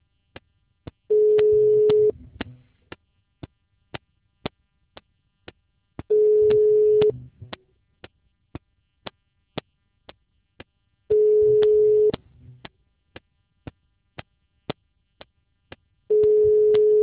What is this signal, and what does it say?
Ringback tone (Europe, 425 Hz).